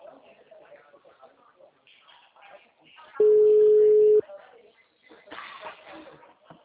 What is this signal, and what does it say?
Ringback tone (Europe, 425 Hz), with speech.
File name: out- -2009- - - .37266